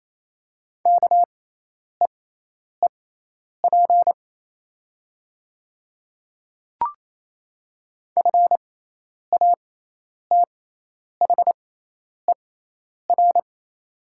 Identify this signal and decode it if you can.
Morse code, heard as a keyed tone, about 28 wpm, spaced as for 10 wpm, 700 Hz. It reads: KEEP FATHER
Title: 6x Character Spacing - Top 300 Words - 28wpm